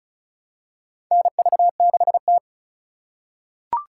Morse code, heard as a keyed tone, about 36 wpm, 700 Hz. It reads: NV6T
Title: USA Call Signs - 4 Characters - 3x Character Spacing - 35wpm